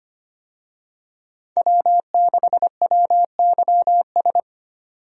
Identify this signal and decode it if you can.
Morse code, heard as a keyed tone, about 25 wpm, 700 Hz. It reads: W6WYS
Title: USA Call Signs - 5 Characters - 25wpm